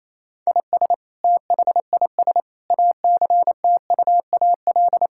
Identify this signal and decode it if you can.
Morse code, heard as a keyed tone, about 28 wpm, 700 Hz. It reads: IS THIS ACTUAL
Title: Sentences from Top 1000 Words - Review Part 5 - 28wpm